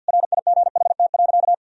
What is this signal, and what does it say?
Morse code, heard as a keyed tone, about 46 wpm, 700 Hz. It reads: RIGHT.
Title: Sentences from Top 1000 Words - Review Part 1 - 50wpm